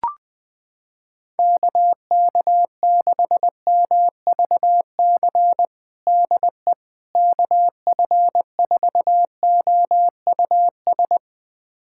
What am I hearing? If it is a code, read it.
Morse code, heard as a keyed tone, about 21 wpm, 700 Hz. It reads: KK6MVC DE KF4OUS
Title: USA Callsign DE Callsign - 6 Characters - 20wpm